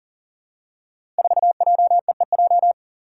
Morse code, heard as a keyed tone, about 42 wpm, 700 Hz. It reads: 4JEEJ
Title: Five Challenging Letter-Number Combinations 40wpm